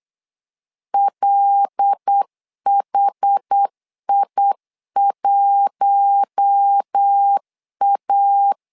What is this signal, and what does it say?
Morse code, heard as a keyed tone, about 8 wpm, 800 Hz. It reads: LHI1A
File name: LEZIONI / 5 LBVC / 5 LB 8wpm 2816